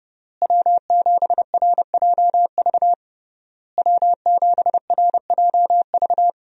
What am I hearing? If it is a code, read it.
Morse code, heard as a keyed tone, about 30 wpm, 700 Hz. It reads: W7RJV W7RJV
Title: USA Call Signs - 5 Characters - Repeated 3x - 30wpm